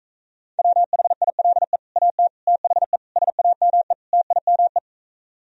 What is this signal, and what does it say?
Morse code, heard as a keyed tone, about 45 wpm, 700 Hz. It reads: WHILE AT THE SAME TIME